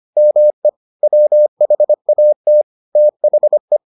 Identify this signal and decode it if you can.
Morse code, heard as a keyed tone, about 25 wpm, 600 Hz. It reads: ME WHAT THE